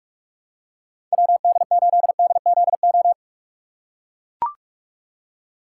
Morse code, heard as a keyed tone, about 43 wpm, 700 Hz. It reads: WD8DZO
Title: USA Call Signs - 6 Characters - 2x Character Spacing - 45wpm